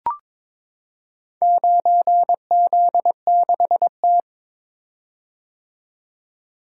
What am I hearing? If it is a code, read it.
Morse code, heard as a keyed tone, about 22 wpm, 700 Hz. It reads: 9Z6T
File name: practice / 22wpm / Four Letter-Number Combinations - 22wpm